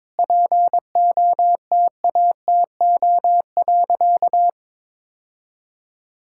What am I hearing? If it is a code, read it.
Morse code, heard as a keyed tone, about 21 wpm, 700 Hz. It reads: POTATO.
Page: morse-code-ninja-practice-22wpm